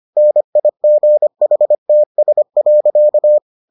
Morse code, heard as a keyed tone, about 25 wpm, 600 Hz. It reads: NIGHTS.